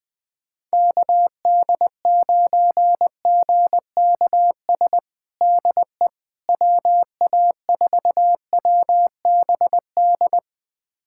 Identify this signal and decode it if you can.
Morse code, heard as a keyed tone, about 20 wpm, 700 Hz. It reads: KD9GKS DE WA4WBD
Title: USA Callsign DE Callsign - 6 Characters - 20wpm